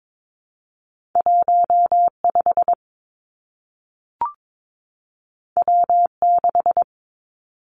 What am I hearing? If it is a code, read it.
Morse code, heard as a keyed tone, about 22 wpm, 700 Hz. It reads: 1 5 W 6
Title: Two Challenging Letter-Number Combinations 22wpm